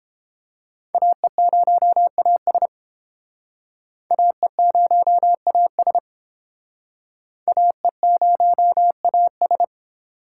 Morse code, heard as a keyed tone, about 30 wpm, 700 Hz. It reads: AE0AS AE0AS AE0AS